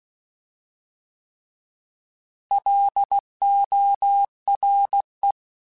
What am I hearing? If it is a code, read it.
Morse code, heard as a keyed tone, about 16 wpm, 800 Hz. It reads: LORE
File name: Alphabetical / Alphabetical mp3s / Ls @15